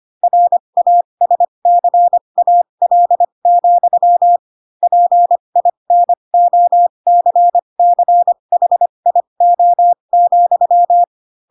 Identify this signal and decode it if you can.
Morse code, heard as a keyed tone, about 25 wpm, spaced as for 22 wpm, 700 Hz. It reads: RASCAL, PINOCCHIO,
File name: Pinocchio / Pinocchio ch4 18 25 ik1hge